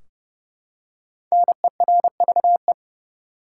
Morse code, heard as a keyed tone, about 30 wpm, 700 Hz. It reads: NERVE
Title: Five Letter Words - 30wpm